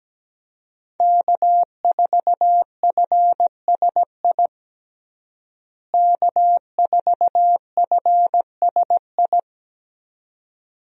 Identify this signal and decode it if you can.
Morse code, heard as a keyed tone, about 17 wpm, 700 Hz. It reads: K4FSI K4FSI
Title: USA Call Signs - 5 Characters - Repeated 2x - 17wpm